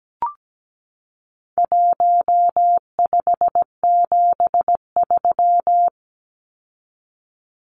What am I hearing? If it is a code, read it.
Morse code, heard as a keyed tone, about 17 wpm, 700 Hz. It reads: 1573
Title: Numbers - Contest Style - 4-Digit 17wpm